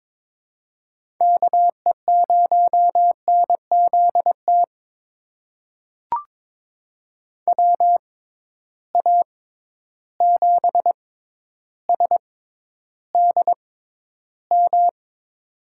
Morse code, heard as a keyed tone, about 22 wpm, spaced as for 11 wpm, 700 Hz. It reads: KE0NZT WA7SDM
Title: USA Call Signs - 6 Characters - 6x Character Spacing - 22wpm